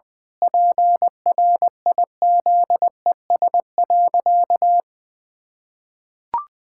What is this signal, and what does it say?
Morse code, heard as a keyed tone, about 20 wpm, 700 Hz. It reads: PRIZES.
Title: Pangram Challenge v2 - 20wpm